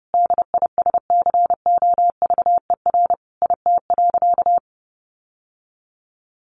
Morse code, heard as a keyed tone, about 30 wpm, 700 Hz. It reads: DISCOVER IT.
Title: Sentences from Top 1100 Words 30wpm